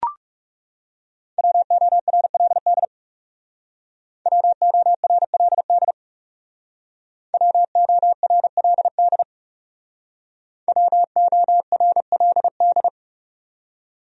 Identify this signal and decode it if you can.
Morse code, heard as a keyed tone, about 39 wpm, 700 Hz. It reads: WORLD WORLD WORLD WORLD